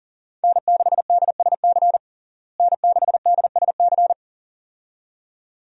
Morse code, heard as a keyed tone, about 38 wpm, 700 Hz. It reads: N6DSC N6DSC
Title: USA Call Signs - 5 Characters - Repeated 2x - 40wpm